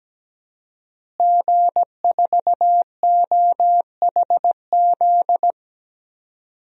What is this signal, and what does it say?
Morse code, heard as a keyed tone, about 17 wpm, 700 Hz. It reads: G4OHZ
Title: DXCC-100 Call Signs - 5 Characters - Repeated 3x - 17wpm